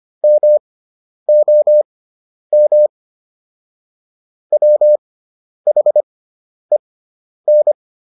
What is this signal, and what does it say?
Morse code, heard as a keyed tone, about 25 wpm, spaced as for 10 wpm, 600 Hz. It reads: MOM WHEN